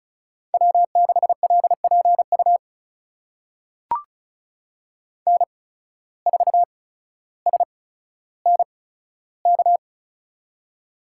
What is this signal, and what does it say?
Morse code, heard as a keyed tone, about 36 wpm, spaced as for 15 wpm, 700 Hz. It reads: W6LPU N4SNK
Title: USA Call Signs - 5 Characters - 8x Character Spacing - 35wpm